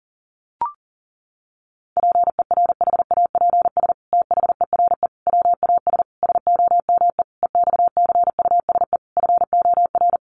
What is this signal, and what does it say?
Morse code, heard as a keyed tone, about 40 wpm, 700 Hz. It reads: PERHAPS THERE WAS SOME EXCUSE FOR
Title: Sentences from Top 700 Words 40wpm